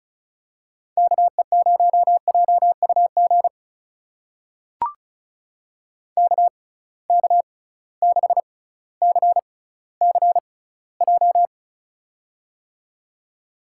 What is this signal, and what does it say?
Morse code, heard as a keyed tone, about 36 wpm, spaced as for 18 wpm, 700 Hz. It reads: KE0JUG KK6CCJ
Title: USA Call Signs - 6 Characters - 6x Character Spacing - 35wpm